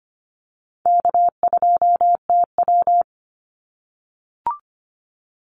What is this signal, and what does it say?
Morse code, heard as a keyed tone, about 25 wpm, 700 Hz. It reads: K2TW
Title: Call Signs - 25wpm